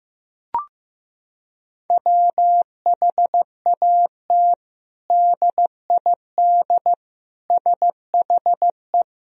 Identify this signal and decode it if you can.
Morse code, heard as a keyed tone, about 15 wpm, 700 Hz. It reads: WHAT DID SHE
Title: Sentences from Top 1000 Words - Review Part 4 - 15wpm